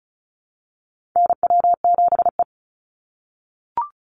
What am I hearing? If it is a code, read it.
Morse code, heard as a keyed tone, about 36 wpm, 700 Hz. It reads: NW7E
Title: NA SOTA Chaser Callsigns 35wpm